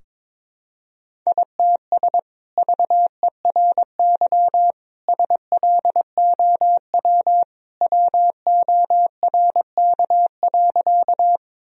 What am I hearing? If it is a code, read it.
Morse code, heard as a keyed tone, about 22 wpm, 700 Hz. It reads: ITS VERY SLOW WORK.
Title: Sentences from Top 1000 Words - Review Part 2 - 22wpm